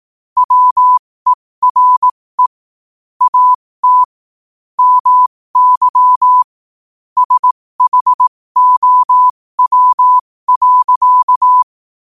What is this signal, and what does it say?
Morse code, heard as a keyed tone, about 18 wpm, spaced as for 15 wpm, 1000 Hz. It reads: WERE AT MY SHOW.
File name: qotdfiles / 15wpm